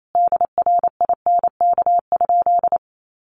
Morse code, heard as a keyed tone, about 28 wpm, 700 Hz. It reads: DRINK?
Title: Sentences from Top 1500 Words 28wpm